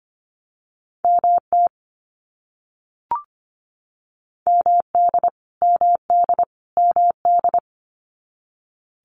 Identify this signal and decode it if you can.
Morse code, heard as a keyed tone, about 25 wpm, 700 Hz. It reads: MT MD MD MD
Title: US States Repeated 3x 25wpm